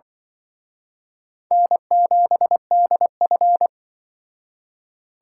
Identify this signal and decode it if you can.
Morse code, heard as a keyed tone, about 23 wpm, 700 Hz. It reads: N7DF
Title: Speed Racing - USA 1x2 Call Signs 20wpm